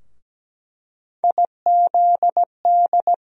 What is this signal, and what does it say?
Morse code, heard as a keyed tone, about 17 wpm, 700 Hz. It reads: IZD
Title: USA Call Sign Postfixes - 3 Letters 17wpm